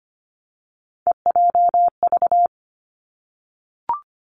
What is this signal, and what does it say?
Morse code, heard as a keyed tone, about 25 wpm, 700 Hz. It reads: EJV
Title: Three Challenging Letter-Number Combinations 25wpm